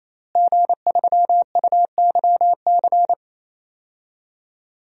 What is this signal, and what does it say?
Morse code, heard as a keyed tone, about 27 wpm, 700 Hz. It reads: G3UYC
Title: DXCC-100 Call Signs - 5 Characters - Repeated 3x - 28wpm